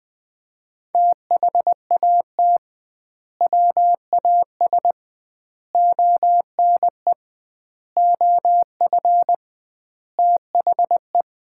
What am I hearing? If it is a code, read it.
Morse code, heard as a keyed tone, about 20 wpm, 700 Hz. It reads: THAT WAS ONE OF THE
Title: 2x Word-Spacing - Sets of 5 Words - 20wpm